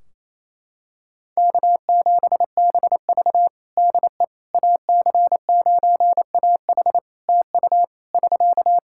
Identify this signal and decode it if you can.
Morse code, heard as a keyed tone, about 28 wpm, 700 Hz. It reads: K7BV DE AC9AH TU <SK>
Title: Practice QSO - Exchange 4 - Ending - 28wpm